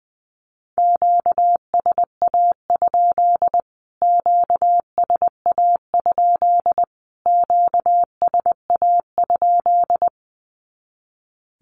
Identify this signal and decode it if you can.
Morse code, heard as a keyed tone, about 20 wpm, 700 Hz. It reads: QSA? QSA? QSA?
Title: Q-Codes Repeated 3x 20wpm